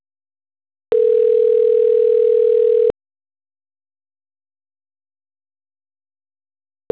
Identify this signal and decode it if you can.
Ringback tone (North America), 440 + 480 Hz.
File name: autotestagent / 10-48-46.136..callcentertransfer. .0cf5218b-b3cf-432e-a464